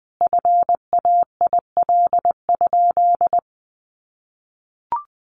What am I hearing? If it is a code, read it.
Morse code, heard as a keyed tone, about 19 wpm, 700 Hz. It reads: FAIL?
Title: Sentences from Top 1400 Words 20wpm